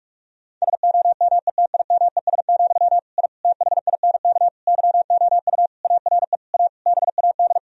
Morse code, heard as a keyed tone, about 46 wpm, 700 Hz. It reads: SOMETIMES, I THINK YOU ARE A BAD
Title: Sentences from Top 1000 Words - Review Part 3 - 45wpm